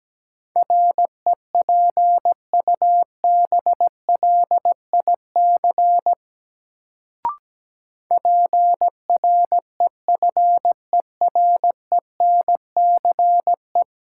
Morse code, heard as a keyed tone, about 17 wpm, 700 Hz. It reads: REPUBLIC PREFERENCE
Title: Sentences from Top 1500 Words - New Word Drill - 17wpm